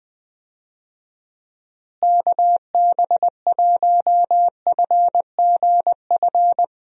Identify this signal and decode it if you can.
Morse code, heard as a keyed tone, about 20 wpm, 700 Hz. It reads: KB1FGF
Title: USA Call Signs - 6 Characters - Repeated 2x - 20wpm